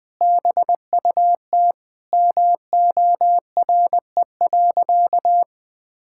Morse code, heard as a keyed tone, about 20 wpm, 700 Hz. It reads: BUT MORE.
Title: Sentences with 2-Word Phrases - 20wpm